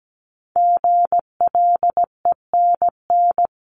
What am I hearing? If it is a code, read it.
Morse code, heard as a keyed tone, about 17 wpm, 700 Hz. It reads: GLENN
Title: CWT Exchange 17wpm